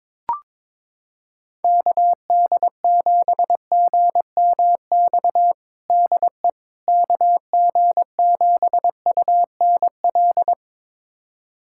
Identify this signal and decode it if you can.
Morse code, heard as a keyed tone, about 22 wpm, 700 Hz. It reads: KD7GMX DE KG7UNL
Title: USA Callsign DE Callsign - 6 Characters - 22wpm